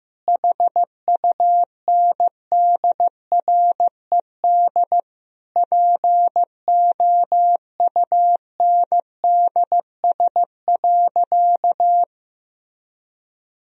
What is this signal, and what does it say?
Morse code, heard as a keyed tone, about 15 wpm, 700 Hz. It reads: HUNDRED POUNDS.
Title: Sentences from Top 1000 Words - Review Part 2 - 15wpm